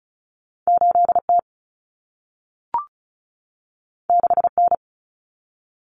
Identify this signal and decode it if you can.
Morse code, heard as a keyed tone, about 36 wpm, 700 Hz. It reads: 8T 6N
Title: Common Cut Numbers - 2-Digit 35wpm